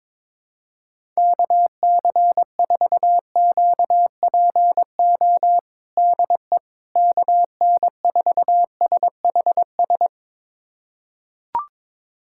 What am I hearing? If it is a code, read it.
Morse code, heard as a keyed tone, about 22 wpm, 700 Hz. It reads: KC4QPO DE KN4SHS
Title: USA Callsign DE Callsign 22wpm